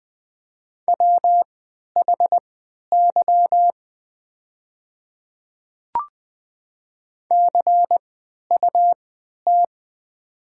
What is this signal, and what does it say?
Morse code, heard as a keyed tone, about 20 wpm, spaced as for 11 wpm, 700 Hz. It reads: WHY CUT
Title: 3x Character Spacing - Top 300 Words - 20wpm